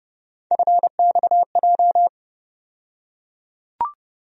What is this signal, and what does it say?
Morse code, heard as a keyed tone, about 32 wpm, 700 Hz. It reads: FXJ
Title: USA Call Sign Postfixes - 3 Letters 30wpm